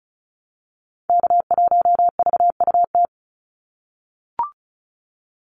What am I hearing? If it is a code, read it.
Morse code, heard as a keyed tone, about 34 wpm, 700 Hz. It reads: K1VUT
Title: Call Signs - 35wpm